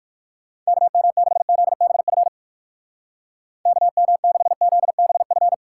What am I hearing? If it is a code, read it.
Morse code, heard as a keyed tone, about 50 wpm, 700 Hz. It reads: KM6ZBF KM6ZBF
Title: Speed Racing - USA 2x3 Call Signs 45wpm